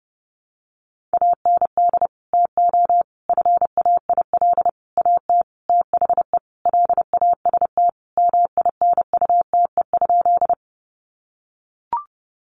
Morse code, heard as a keyed tone, about 29 wpm, 700 Hz. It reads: AND TO FAIL AT THE LAST MINUTE?
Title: Sentences from Top 1100 Words 30wpm